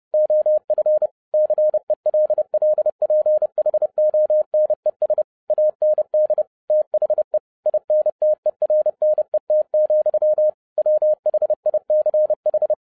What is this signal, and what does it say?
Morse code, heard as a keyed tone, about 30 wpm, 610 Hz. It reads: OF CELLPHONES AND THE INTERNET, WHICH